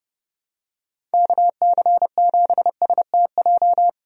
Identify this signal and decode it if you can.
Morse code, heard as a keyed tone, about 32 wpm, 700 Hz. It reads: KC7STJ
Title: USA Call Signs - 6 Characters - 30wpm